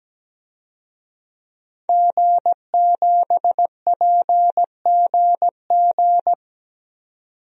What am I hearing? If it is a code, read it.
Morse code, heard as a keyed tone, about 17 wpm, 700 Hz. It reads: G7PGG
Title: DXCC-100 Call Signs - 5 Characters - 17wpm